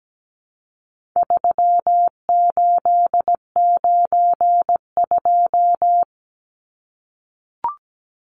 Morse code, heard as a keyed tone, about 17 wpm, 700 Hz. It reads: 3892
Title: Numbers - 4-Digit 17wpm